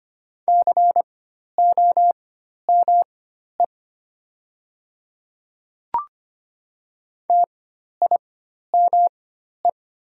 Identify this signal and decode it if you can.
Morse code, heard as a keyed tone, about 25 wpm, spaced as for 12 wpm, 700 Hz. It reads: COME TIME
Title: 4x Character Spacing - Top 100 Words - 25wpm